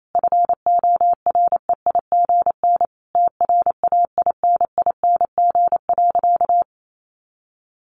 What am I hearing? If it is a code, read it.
Morse code, heard as a keyed tone, about 28 wpm, 700 Hz. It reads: FOREIGN TRAINING.
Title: Sentences from Top 1400 Words 28wpm